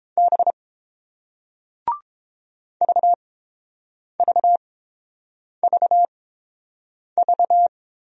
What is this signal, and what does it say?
Morse code, heard as a keyed tone, about 30 wpm, spaced as for 7 wpm, 690 Hz. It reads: B VVVV